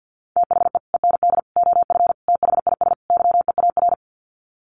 Morse code, heard as a keyed tone, about 48 wpm, 700 Hz. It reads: THE END OF THIS YEAR
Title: Sets of 5 Words Encore 50wpm